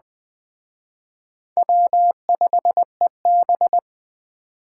Morse code, heard as a keyed tone, about 21 wpm, 700 Hz. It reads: W5EB